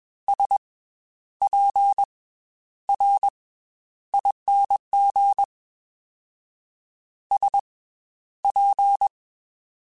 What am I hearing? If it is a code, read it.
Morse code, heard as a keyed tone, about 21 wpm, 800 Hz. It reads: S P R ING S P
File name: Special INGs / Special INGs-mp3s / Slow ing @20